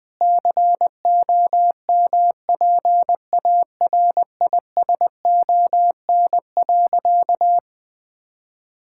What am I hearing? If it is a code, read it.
Morse code, heard as a keyed tone, about 21 wpm, 700 Hz. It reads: COMPARISON.
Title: Sentences from Top 1600 Words 20wpm